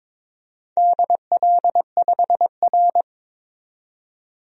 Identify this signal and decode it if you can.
Morse code, heard as a keyed tone, about 22 wpm, 700 Hz. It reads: DL5R
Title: DXCC-100 Call Signs - 4 Characters - 22wpm